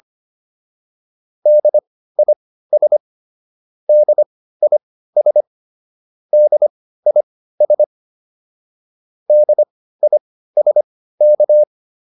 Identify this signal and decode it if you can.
Morse code, heard as a keyed tone, about 25 wpm, spaced as for 15 wpm, 600 Hz. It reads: DIS DIS DIS DISK